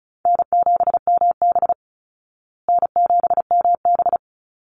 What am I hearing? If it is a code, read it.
Morse code, heard as a keyed tone, about 34 wpm, 700 Hz. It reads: N7MB N7MB
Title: Easy Call Signs Repeated 2x - 35wpm